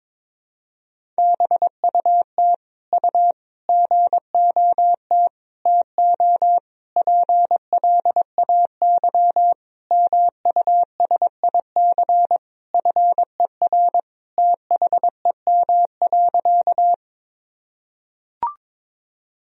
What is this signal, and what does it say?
Morse code, heard as a keyed tone, about 22 wpm, 700 Hz. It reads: BUT U GOT TO PLAY MUSIC FER THEM.
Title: Sentences from Top 300 Words and Abbreviations 22wpm